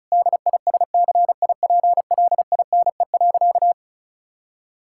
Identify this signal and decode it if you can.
Morse code, heard as a keyed tone, about 36 wpm, 700 Hz. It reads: DISCIPLINE.
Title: Sentences from Top 1000 Words - Review Part 6 - 35wpm